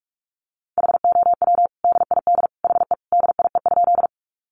Morse code, heard as a keyed tone, about 44 wpm, 700 Hz. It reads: HOW DID HE DIE?
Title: Sentences from Top 1300 Words 45wpm